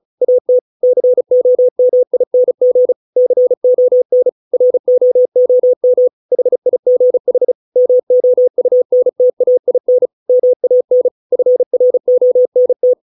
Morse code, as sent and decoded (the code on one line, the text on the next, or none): .- - / -.-. --- -- .. -. --. / -.-. --- -. / .-. --- --- -- / .... .. --. .... / -- --- ..- -. - .- .. -. / -- .- -. / ..-. .-. --- -. -
AT COMING CON ROOM HIGH MOUNTAIN MAN FRONT